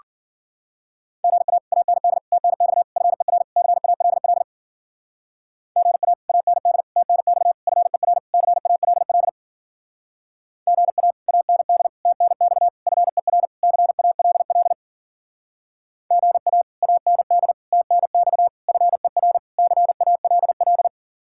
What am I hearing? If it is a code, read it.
Morse code, heard as a keyed tone, about 49 wpm, 700 Hz. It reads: GA AND TNX FER CALL GA AND TNX FER CALL GA AND TNX FER CALL GA AND TNX FER CALL
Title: Speed-Racing Common QSO Phrases - 40wpm